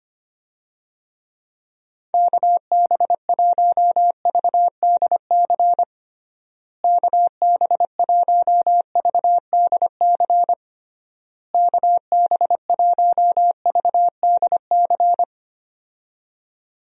Morse code, heard as a keyed tone, about 25 wpm, 700 Hz. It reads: KB1VDC KB1VDC KB1VDC
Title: USA Call Signs - 6 Characters - Repeated 3x - 25wpm